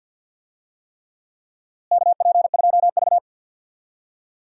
Morse code, heard as a keyed tone, about 52 wpm, 700 Hz. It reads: KP2V
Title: DXCC-100 Call Signs - 4 Characters - Repeated 3x - 50wpm